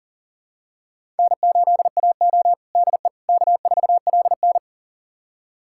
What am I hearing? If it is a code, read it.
Morse code, heard as a keyed tone, about 37 wpm, 700 Hz. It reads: N8AO DE K4LN
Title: USA Callsign DE Callsign - 4 Characters - 40wpm